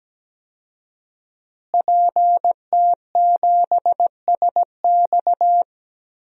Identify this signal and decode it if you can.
Morse code, heard as a keyed tone, about 17 wpm, 700 Hz. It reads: PT7SX
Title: DXCC-100 Call Signs - 5 Characters - Repeated 2x - 17wpm